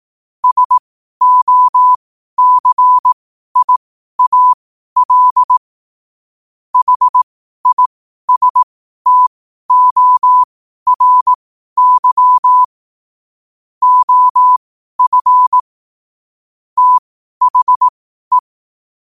Morse code, heard as a keyed tone, about 18 wpm, spaced as for 12 wpm, 1000 Hz. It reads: SOCIAL HISTORY OF THE
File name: qotdfiles / 12wpm